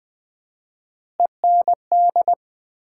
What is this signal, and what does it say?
Morse code, heard as a keyed tone, about 20 wpm, 700 Hz. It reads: END
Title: 4x Word-Spacing - Sets of 5 Words - 20wpm